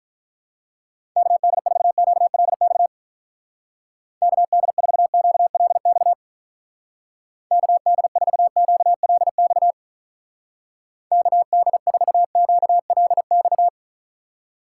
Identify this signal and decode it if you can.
Morse code, heard as a keyed tone, about 43 wpm, 700 Hz. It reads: KD4QLX KD4QLX KD4QLX KD4QLX
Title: Speed Racing - USA 2x3 Call Signs 35wpm